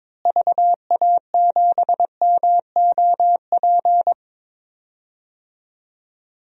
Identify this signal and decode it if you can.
Morse code, heard as a keyed tone, about 21 wpm, 700 Hz. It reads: VA7MOP
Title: DXCC-100 Call Signs - 6 Characters - Repeated 3x - 22wpm